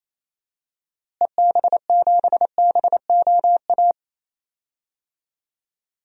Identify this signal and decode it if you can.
Morse code, heard as a keyed tone, about 28 wpm, 700 Hz. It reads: EB7BOA
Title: DXCC-100 Call Signs - 6 Characters - Repeated 3x - 28wpm